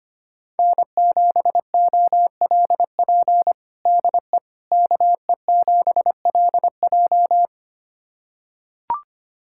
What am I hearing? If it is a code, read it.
Morse code, heard as a keyed tone, about 25 wpm, 700 Hz. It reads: N7OLP DE KE7LJ
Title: USA Callsign DE Callsign - 5 Characters - 25wpm